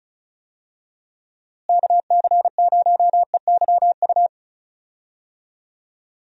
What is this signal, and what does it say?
Morse code, heard as a keyed tone, about 34 wpm, 700 Hz. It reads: KC0EYU
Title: USA Call Signs - 6 Characters - Repeated 3x - 35wpm